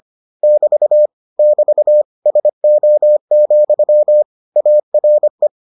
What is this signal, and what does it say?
Morse code, heard as a keyed tone, about 25 wpm, 600 Hz. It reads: <BT> <BT> SO, ARE